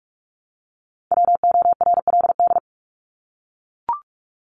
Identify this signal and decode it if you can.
Morse code, heard as a keyed tone, about 44 wpm, 700 Hz. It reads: WORLD
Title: Top 200 Words 45wpm